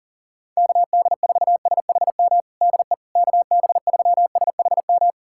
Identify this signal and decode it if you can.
Morse code, heard as a keyed tone, about 40 wpm, 700 Hz. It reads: KD4SHM DE KB3SHM
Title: USA Callsign DE Callsign - 6 Characters - 40wpm